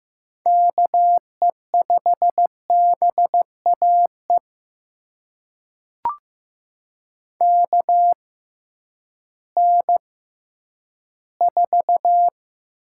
Morse code, heard as a keyed tone, about 15 wpm, spaced as for 10 wpm, 700 Hz. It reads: KE5BAE KN4
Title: USA Call Signs - 6 Characters - 6x Character Spacing - 15wpm